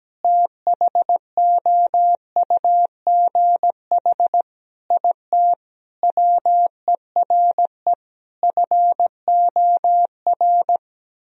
Morse code, heard as a keyed tone, about 17 wpm, 700 Hz. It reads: THOUGH IT WERE FOR
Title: Sentences from Top 1000 Words - Review Part 2 - 17wpm